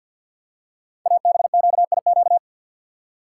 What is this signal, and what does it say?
Morse code, heard as a keyed tone, about 50 wpm, 690 Hz. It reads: ABQIQ
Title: Code Groups - 5 Letters - 50wpm